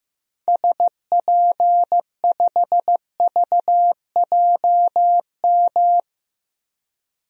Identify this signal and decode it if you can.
Morse code, heard as a keyed tone, about 15 wpm, 700 Hz. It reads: SP5VJM
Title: DXCC-100 Call Signs - 6 Characters - 15wpm